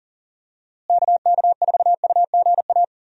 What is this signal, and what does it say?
Morse code, heard as a keyed tone, about 40 wpm, 700 Hz. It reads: KK4UGA
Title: USA Call Signs - 6 Characters - Repeated 3x - 40wpm